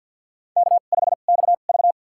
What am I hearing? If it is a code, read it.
Morse code, heard as a keyed tone, about 53 wpm, spaced as for 37 wpm, 700 Hz. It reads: K5XV
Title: USA Call Signs - 4 Characters - 2x Character Spacing - 50wpm